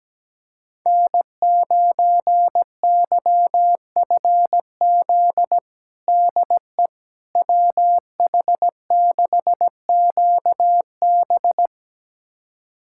Morse code, heard as a keyed tone, about 17 wpm, 700 Hz. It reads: N9YFZ DE WH6QB
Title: USA Callsign DE Callsign - 5 Characters - 17wpm